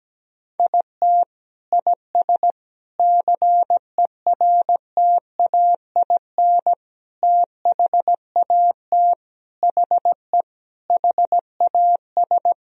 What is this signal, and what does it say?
Morse code, heard as a keyed tone, about 17 wpm, 700 Hz. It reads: IT IS CERTAIN THAT HE HAS
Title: Sentences from Top 1000 Words - Review Part 2 - 17wpm